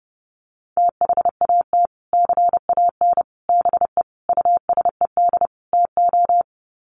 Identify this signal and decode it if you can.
Morse code, heard as a keyed tone, about 30 wpm, 700 Hz. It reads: THAT CAN BE USED TO
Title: Sets of 5 Words Encore 30wpm